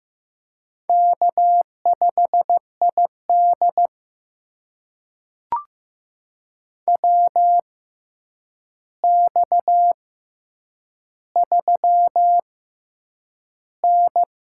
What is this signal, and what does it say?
Morse code, heard as a keyed tone, about 15 wpm, spaced as for 8 wpm, 700 Hz. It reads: K5ID WX3N
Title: USA Call Signs - 4 Characters - 6x Character Spacing - 15wpm